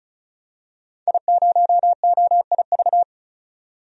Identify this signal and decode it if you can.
Morse code, heard as a keyed tone, about 36 wpm, 700 Hz. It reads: I0OIV
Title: DXCC-100 Call Signs - 5 Characters - 35wpm